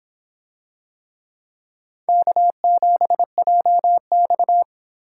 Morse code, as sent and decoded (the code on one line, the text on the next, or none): -.- --... .--- -..-
K7JX